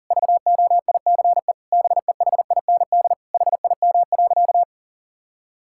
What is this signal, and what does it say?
Morse code, heard as a keyed tone, about 37 wpm, 700 Hz. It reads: VOICE BEHIND HIM.
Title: Sentences from Top 1600 Words 40wpm